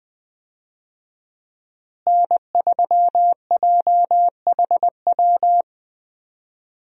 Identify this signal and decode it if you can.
Morse code, heard as a keyed tone, about 20 wpm, 700 Hz. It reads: N3JHW
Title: USA Call Signs - 5 Characters - Repeated 2x - 20wpm